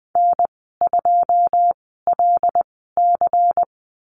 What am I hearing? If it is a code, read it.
Morse code, heard as a keyed tone, about 20 wpm, spaced as for 15 wpm, 700 Hz. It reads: N2LC
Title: 167 Lesson 27g - Number 7 - Call Signs 20wpm